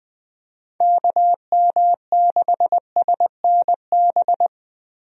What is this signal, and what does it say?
Morse code, heard as a keyed tone, about 20 wpm, 700 Hz. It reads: KM6SNB
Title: USA Call Signs - 6 Characters - Repeated 2x - 20wpm